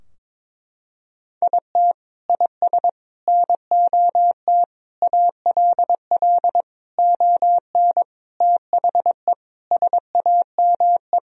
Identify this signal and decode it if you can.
Morse code, heard as a keyed tone, about 22 wpm, 700 Hz. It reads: IT IS NOT ALL ON THE SAME
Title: Sentences from Top 1000 Words - Review Part 3 - 22wpm